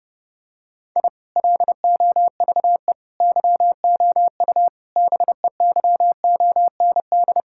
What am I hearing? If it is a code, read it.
Morse code, heard as a keyed tone, about 28 wpm, 700 Hz. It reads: I LOVE YOU BEYOND
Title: Sentences from Top 1000 Words - Review Part 6 - 30wpm